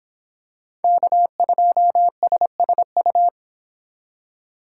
Morse code, heard as a keyed tone, about 26 wpm, 700 Hz. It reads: K2SSU